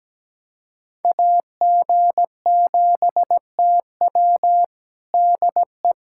Morse code, heard as a keyed tone, about 17 wpm, 700 Hz. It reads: AG7TW DE
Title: USA Callsign DE Callsign - 5 Characters - 17wpm